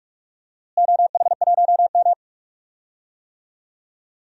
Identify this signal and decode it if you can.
Morse code, heard as a keyed tone, about 46 wpm, 700 Hz. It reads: OH1M
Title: DXCC-100 Call Signs - 4 Characters - Repeated 3x - 45wpm